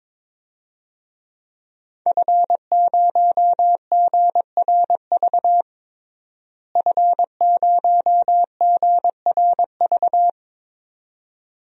Morse code, heard as a keyed tone, about 21 wpm, 700 Hz. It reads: F0GRV F0GRV
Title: DXCC-100 Call Signs - 5 Characters - Repeated 2x - 22wpm